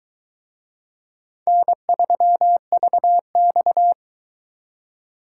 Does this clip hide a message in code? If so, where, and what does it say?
Yes: Morse code from 1.5 to 3.9 s, heard as a keyed tone, about 23 wpm, 700 Hz. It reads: N3VX